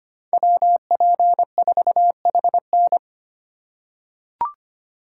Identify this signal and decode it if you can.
Morse code, heard as a keyed tone, about 25 wpm, 700 Hz. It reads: WP4HN
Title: USA Call Signs - 5 Characters - 4x Character Spacing - 25wpm